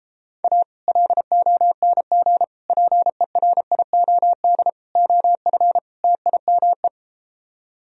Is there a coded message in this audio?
Yes: Morse code from 0.4 to 6.9 s, heard as a keyed tone, about 33 wpm, 700 Hz. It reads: A LONG PERIOD OF TIME